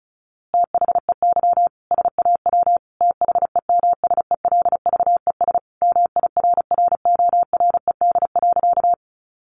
Morse code, heard as a keyed tone, about 34 wpm, 700 Hz. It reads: THEY SAW THEMSELVES MIRRORED.
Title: A Sound Of Thunder 35wpm